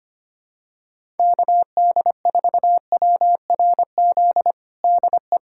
Morse code, heard as a keyed tone, about 25 wpm, 700 Hz. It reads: KD4WRZ DE
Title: USA Callsign DE Callsign - 6 Characters - 25wpm